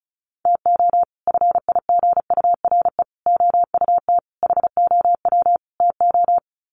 Morse code, heard as a keyed tone, about 34 wpm, 700 Hz. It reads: TO FIGURE OUT HOW TO
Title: Sets of 5 Words 35wpm